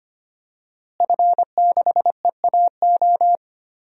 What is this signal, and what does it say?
Morse code, heard as a keyed tone, about 25 wpm, 700 Hz. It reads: F6EAO
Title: DXCC-100 Call Signs - 5 Characters - Repeated 2x - 25wpm